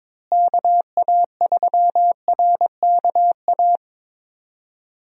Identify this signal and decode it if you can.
Morse code, heard as a keyed tone, about 21 wpm, 700 Hz. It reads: KA3RKA
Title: USA Call Signs - 6 Characters - Repeated 2x - 22wpm